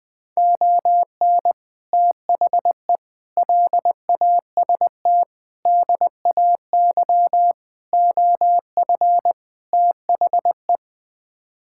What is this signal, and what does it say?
Morse code, heard as a keyed tone, about 20 wpm, 700 Hz. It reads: ON THE LAST DAY OF THE
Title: Sets of 6 Words - 20wpm